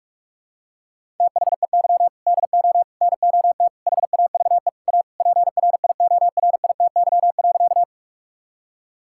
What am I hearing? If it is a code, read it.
Morse code, heard as a keyed tone, about 46 wpm, 700 Hz. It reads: THEY DO NOT HAVE A PRIORITY.
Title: Sentences from Top 1600 Words 45wpm